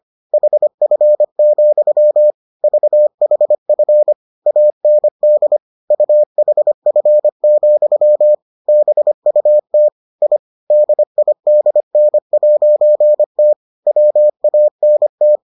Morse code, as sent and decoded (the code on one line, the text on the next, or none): .... ..-. --..-- / ...- .... ..-. / .- -. -.. / ..- .... ..-. --..-- / -... ..- - / .. / -.. .. -.. -. .----. - / .-- .- -. -
HF, VHF AND UHF, BUT I DIDN'T WANT